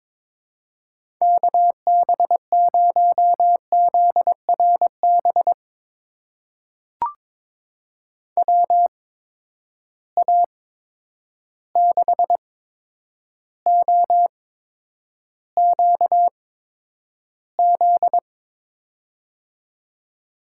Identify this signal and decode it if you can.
Morse code, heard as a keyed tone, about 22 wpm, spaced as for 9 wpm, 700 Hz. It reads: KB0ZRB WA6OQZ
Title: USA Call Signs - 6 Characters - 8x Character Spacing - 22wpm